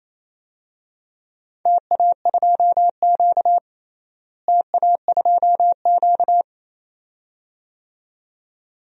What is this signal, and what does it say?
Morse code, heard as a keyed tone, about 28 wpm, 700 Hz. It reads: TA2Q TA2Q
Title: DXCC-100 Call Signs - 4 Characters - Repeated 2x - 28wpm